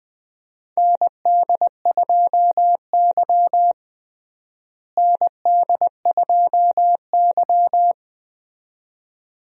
Morse code, heard as a keyed tone, about 21 wpm, 700 Hz. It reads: ND2Y ND2Y
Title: USA Call Signs - 4 Characters - Repeated 2x - 20wpm